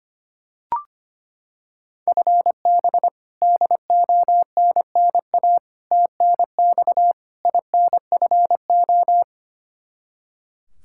Morse code, heard as a keyed tone, about 25 wpm, 700 Hz. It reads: FB DONNA TNX INFO
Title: Practice QSO - Exchange 3 - More Info - 25wpm